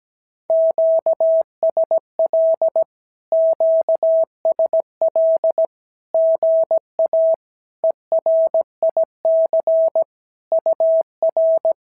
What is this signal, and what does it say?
Morse code, heard as a keyed tone, about 17 wpm, 650 Hz. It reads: QSL QSL GA ERIC UR
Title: SOTA QSO Practice 17wpm